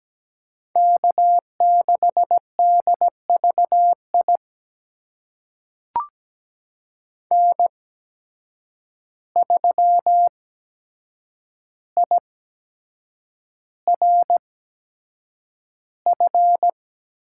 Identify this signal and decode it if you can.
Morse code, heard as a keyed tone, about 17 wpm, spaced as for 7 wpm, 700 Hz. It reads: K6DVI N3IRF
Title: USA Call Signs - 5 Characters - 8x Character Spacing - 17wpm